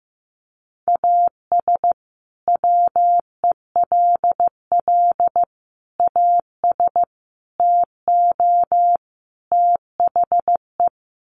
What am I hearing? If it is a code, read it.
Morse code, heard as a keyed tone, about 15 wpm, 700 Hz. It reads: AS WELL AS TO THE
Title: Sets of 5 Words Encore2 15wpm